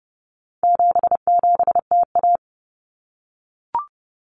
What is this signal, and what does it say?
Morse code, heard as a keyed tone, about 30 wpm, 700 Hz. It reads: 77TA
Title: Common Cut Numbers - 4-Digit 30wpm